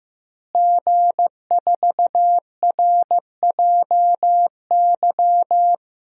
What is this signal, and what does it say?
Morse code, heard as a keyed tone, about 15 wpm, 700 Hz. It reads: G4RJY
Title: DXCC-100 Call Signs - 5 Characters - Repeated 3x - 15wpm